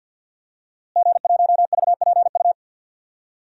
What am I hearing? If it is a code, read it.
Morse code, heard as a keyed tone, about 53 wpm, 700 Hz. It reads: G1VPU
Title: DXCC-100 Call Signs - 5 Characters - Repeated 3x - 50wpm